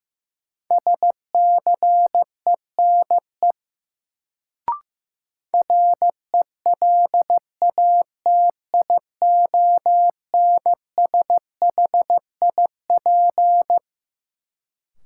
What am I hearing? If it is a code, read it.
Morse code, heard as a keyed tone, about 15 wpm, 700 Hz. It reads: SCENE RELATIONSHIP
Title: Sentences from Top 1100 Words - New Word Drill Alt - 15wpm